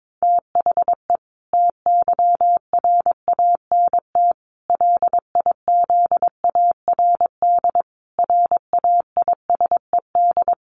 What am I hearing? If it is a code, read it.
Morse code, heard as a keyed tone, about 22 wpm, 700 Hz. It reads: THE TYRANT LIZARD RAISED
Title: A Sound Of Thunder 22wpm